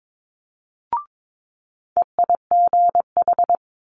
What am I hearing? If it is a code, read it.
Morse code, heard as a keyed tone, about 22 wpm, 700 Hz. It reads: EIGH
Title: Multi-letter Phonemes Quick Fire 22wpm